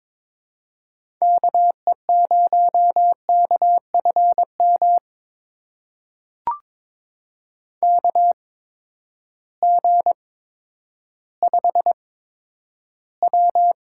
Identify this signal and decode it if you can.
Morse code, heard as a keyed tone, about 22 wpm, spaced as for 11 wpm, 700 Hz. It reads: KE0KFM KG5W
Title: USA Call Signs - 6 Characters - 8x Character Spacing - 22wpm